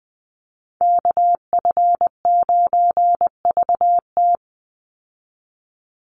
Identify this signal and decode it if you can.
Morse code, heard as a keyed tone, about 20 wpm, 700 Hz. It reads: KF9VT
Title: USA Call Signs 20wpm